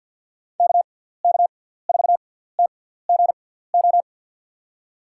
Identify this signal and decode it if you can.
Morse code, heard as a keyed tone, about 47 wpm, spaced as for 17 wpm, 700 Hz. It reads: KK4TGO
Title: USA Call Signs - 6 Characters - 6x Character Spacing - 50wpm